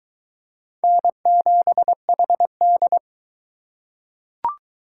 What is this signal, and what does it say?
Morse code, heard as a keyed tone, about 22 wpm, 700 Hz. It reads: N7HD